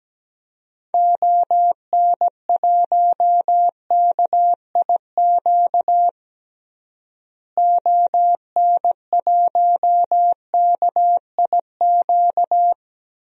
Morse code, heard as a keyed tone, about 17 wpm, 700 Hz. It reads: ON1KIQ ON1KIQ
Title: DXCC-100 Call Signs - 6 Characters - Repeated 2x - 17wpm